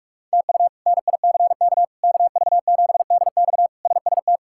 Morse code, heard as a keyed tone, about 46 wpm, 700 Hz. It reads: TU NICK KV7DX SST
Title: K1USN SST QSO Practice - 45wpm